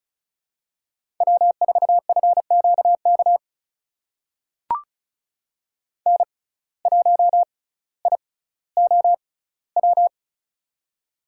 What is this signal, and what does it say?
Morse code, heard as a keyed tone, about 36 wpm, spaced as for 18 wpm, 700 Hz. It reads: W4FQK N1IOW
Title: USA Call Signs - 6x Character Spacing - 35wpm